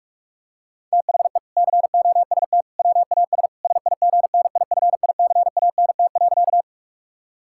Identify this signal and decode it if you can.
Morse code, heard as a keyed tone, about 44 wpm, 700 Hz. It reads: THE COST WAS SIGNIFICANT.
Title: Sentences from Top 1000 Words - Review Part 6 - 45wpm